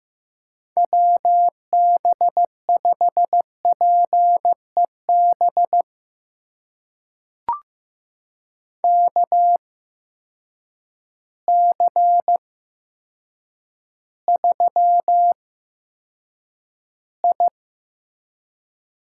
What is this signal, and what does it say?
Morse code, heard as a keyed tone, about 16 wpm, spaced as for 8 wpm, 700 Hz. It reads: WB5PEB KC3I
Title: USA Call Signs - 6 Characters - 8x Character Spacing - 15wpm